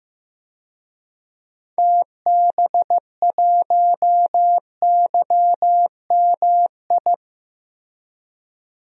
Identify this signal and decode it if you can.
Morse code, heard as a keyed tone, about 16 wpm, 700 Hz. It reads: TB1YMI
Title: DXCC-100 Call Signs - 6 Characters - Repeated 2x - 15wpm